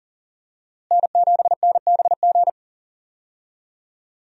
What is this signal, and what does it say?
Morse code, heard as a keyed tone, about 37 wpm, 700 Hz. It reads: N7NBG
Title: USA Call Signs - 5 Characters - Repeated 2x - 40wpm